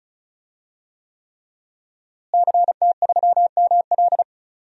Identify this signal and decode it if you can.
Morse code, heard as a keyed tone, about 36 wpm, 700 Hz. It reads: CT3ML
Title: DXCC-100 Call Signs - 5 Characters - Repeated 3x - 35wpm